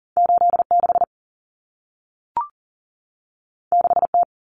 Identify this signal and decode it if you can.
Morse code, heard as a keyed tone, about 40 wpm, 700 Hz. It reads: 86 6T
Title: Common Cut Numbers - 2-Digit 40wpm